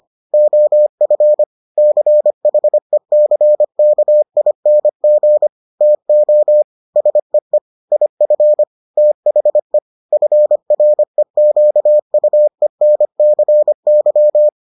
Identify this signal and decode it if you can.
Morse code, heard as a keyed tone, about 25 wpm, 600 Hz. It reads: OF CHECKING TO SEE IF THE FREQUENCY